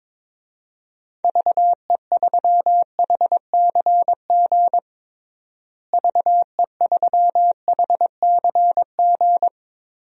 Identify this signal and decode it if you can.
Morse code, heard as a keyed tone, about 21 wpm, 700 Hz. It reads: VE3HCG VE3HCG
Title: DXCC-100 Call Signs - 6 Characters - Repeated 3x - 22wpm